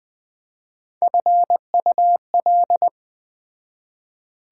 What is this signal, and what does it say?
Morse code, heard as a keyed tone, about 21 wpm, 700 Hz. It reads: FUL